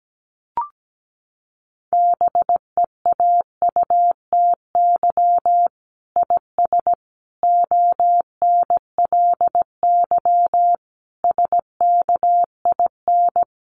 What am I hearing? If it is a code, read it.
Morse code, heard as a keyed tone, about 17 wpm, 700 Hz. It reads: BEAUTY IS ONLY SKIN
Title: Sentences from Top 1300 Words 17wpm